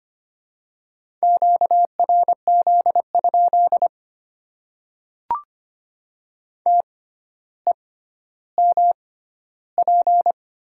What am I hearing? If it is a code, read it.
Morse code, heard as a keyed tone, about 25 wpm, spaced as for 13 wpm, 700 Hz. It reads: QRZ? TEMP
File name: Top 90 QSO Elements - 6x Character Spacing - 25wpm